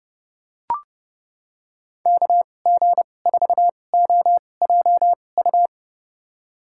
Morse code, heard as a keyed tone, about 32 wpm, spaced as for 22 wpm, 700 Hz. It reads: KG4OJU
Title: USA Call Signs - 6 Characters - 2x Character Spacing - 30wpm